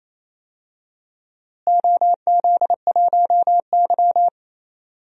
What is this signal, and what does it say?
Morse code, heard as a keyed tone, about 27 wpm, 700 Hz. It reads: OZ1Y
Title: DXCC-100 Call Signs - 4 Characters - Repeated 3x - 28wpm